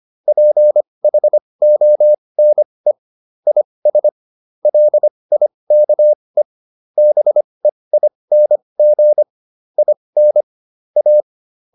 Morse code, heard as a keyed tone, about 25 wpm, spaced as for 20 wpm, 600 Hz. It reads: PHONE IS LIKE BEING IN A